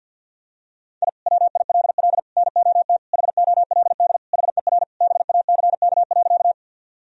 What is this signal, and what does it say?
Morse code, heard as a keyed tone, about 46 wpm, 700 Hz. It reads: I WILL NOT HOLD HER BACK.
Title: Sentences from Top 1000 Words - Review Part 2 - 50wpm